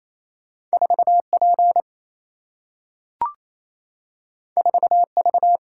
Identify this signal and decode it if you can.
Morse code, heard as a keyed tone, about 28 wpm, 700 Hz. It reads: 4 P 4 V
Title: Single-Letter Single-Number 28wpm